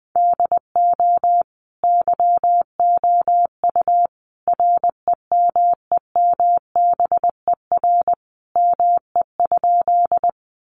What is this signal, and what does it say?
Morse code, heard as a keyed tone, about 20 wpm, 700 Hz. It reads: DO YOU REMEMBER ME?
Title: Sentences from Top 400 Words 20wpm